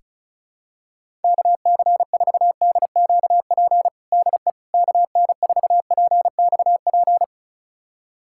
Morse code, heard as a keyed tone, about 36 wpm, 700 Hz. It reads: KC4DQP DE KN4PXP
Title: USA Callsign DE Callsign 35wpm